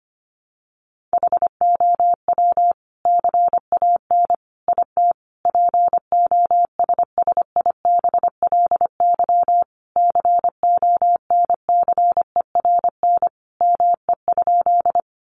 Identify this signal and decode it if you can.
Morse code, heard as a keyed tone, about 25 wpm, 700 Hz. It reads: HOW CAN IT POSSIBLY CONCERN ME?
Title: Sentences from Top 1300 Words 25wpm